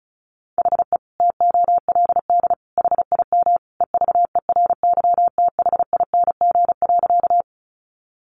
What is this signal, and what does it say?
Morse code, heard as a keyed tone, about 34 wpm, 700 Hz. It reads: HE TOLD HIM EVERYTHING.
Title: Sentences from Top 700 Words 35wpm